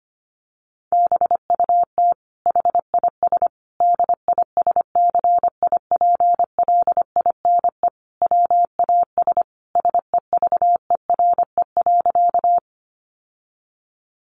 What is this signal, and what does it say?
Morse code, heard as a keyed tone, about 25 wpm, 700 Hz. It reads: BUT HIS DISCIPLINE WAS SEVERE.
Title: Sentences from Top 1400 Words 25wpm